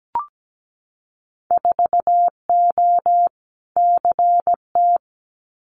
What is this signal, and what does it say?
Morse code, heard as a keyed tone, about 17 wpm, 700 Hz. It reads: 4O CT
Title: Winter Field Day Exchanges Encore 17wpm